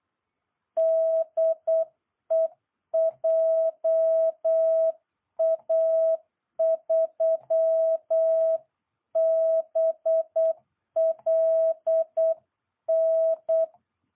Morse code, heard as a keyed tone, about 8 wpm, 660 Hz. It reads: DE JA3BLN